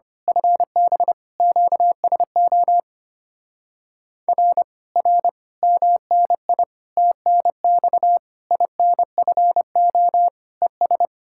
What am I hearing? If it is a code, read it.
Morse code, heard as a keyed tone, about 25 wpm, 700 Hz. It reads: FB QSO R R MNI TNX INFO ES
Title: Speed-Racing Common QSO Phrases Encore - 25wpm